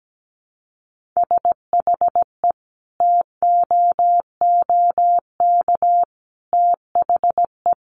Morse code, heard as a keyed tone, about 17 wpm, 700 Hz. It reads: SHE TOOK THE
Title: Sentences from Top 1400 Words 17wpm